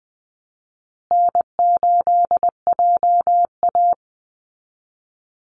Morse code, heard as a keyed tone, about 20 wpm, 700 Hz. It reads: N8JA
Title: Call Signs Repeated 2x - 20wpm